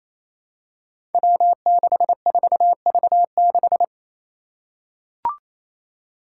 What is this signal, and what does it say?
Morse code, heard as a keyed tone, about 28 wpm, 700 Hz. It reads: W64V6
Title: Five Challenging Letter-Number Combinations 28wpm